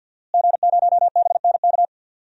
Morse code, heard as a keyed tone, about 52 wpm, 700 Hz. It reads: G0BNK
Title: DXCC-100 Call Signs - 5 Characters - Repeated 2x - 50wpm